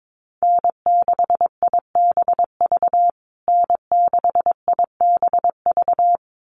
Morse code, heard as a keyed tone, about 22 wpm, 700 Hz. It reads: N6IBV N6IBV
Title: USA Call Signs Repeated 2x 22wpm